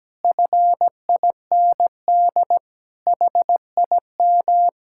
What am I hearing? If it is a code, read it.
Morse code, heard as a keyed tone, about 17 wpm, 700 Hz. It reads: FIND HIM
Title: Sentences from Top 1600 Words 17wpm